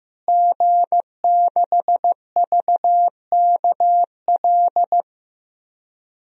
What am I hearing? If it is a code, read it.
Morse code, heard as a keyed tone, about 16 wpm, 700 Hz. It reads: G6VKL
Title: DXCC-100 Call Signs - 5 Characters - Repeated 3x - 15wpm